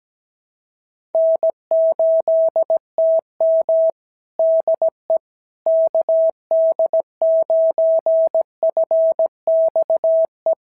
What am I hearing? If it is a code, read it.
Morse code, heard as a keyed tone, about 17 wpm, 650 Hz. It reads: N8TM DE KD9FXE
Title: SOTA QSO Practice 17wpm